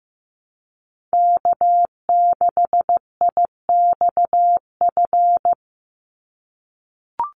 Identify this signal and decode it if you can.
Morse code, heard as a keyed tone, about 15 wpm, 700 Hz. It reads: K6IXF
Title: USA Call Signs 15wpm